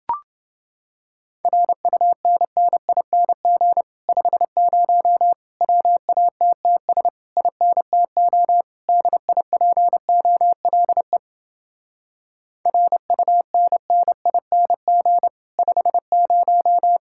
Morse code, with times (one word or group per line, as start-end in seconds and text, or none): 1.4-3.8 RUNNING
4.1-5.3 50
5.6-7.1 WATTS
7.4-8.6 INTO
8.9-11.2 DIPOLE
12.7-15.3 RUNNING
15.6-17.0 50